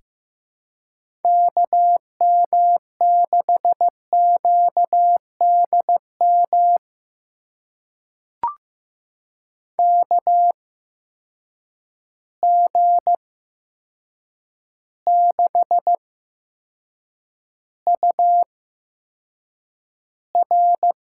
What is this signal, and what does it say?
Morse code, heard as a keyed tone, about 15 wpm, spaced as for 7 wpm, 700 Hz. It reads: KM6QDM KG6UR
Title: USA Call Signs - 6 Characters - 8x Character Spacing - 15wpm